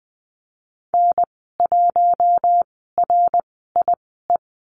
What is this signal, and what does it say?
Morse code, heard as a keyed tone, about 20 wpm, spaced as for 14 wpm, 700 Hz. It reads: N1RIE
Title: 082 Lesson 14g - Number 2 - Call Signs 20wpm